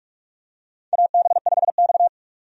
Morse code, heard as a keyed tone, about 46 wpm, 700 Hz. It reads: AB5X
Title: USA Call Signs - 4 Characters - 45wpm